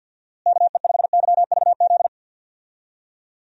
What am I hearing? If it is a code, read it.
Morse code, heard as a keyed tone, about 51 wpm, 700 Hz. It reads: KE5YVZ
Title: USA Call Signs - 6 Characters - Repeated 3x - 50wpm